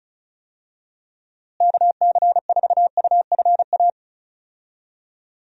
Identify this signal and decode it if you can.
Morse code, heard as a keyed tone, about 34 wpm, 700 Hz. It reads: KC4UFA
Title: USA Call Signs - 6 Characters - Repeated 2x - 35wpm